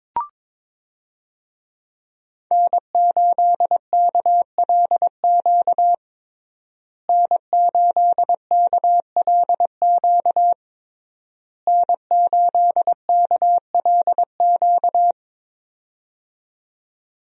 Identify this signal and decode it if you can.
Morse code, heard as a keyed tone, about 22 wpm, 690 Hz. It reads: N8KLQ N8KLQ N8KLQ
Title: USA Call Signs - 5 Characters - Repeated 3x - 22wpm